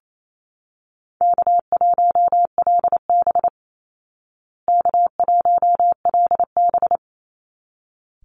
Morse code, heard as a keyed tone, about 28 wpm, 700 Hz. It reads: K1LB K1LB
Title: Easy Call Signs Repeated 2x - 28wpm